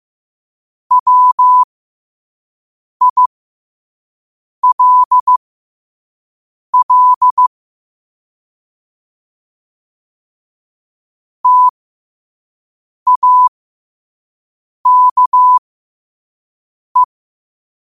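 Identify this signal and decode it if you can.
Morse code, heard as a keyed tone, about 15 wpm, spaced as for 5 wpm, 1000 Hz. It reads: WILL TAKE
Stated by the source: Morse code practice at 5 wpm.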